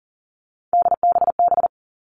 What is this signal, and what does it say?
Morse code, heard as a keyed tone, about 40 wpm, 700 Hz. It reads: DBB
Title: All Cut Numbers - 3-Digit 40wpm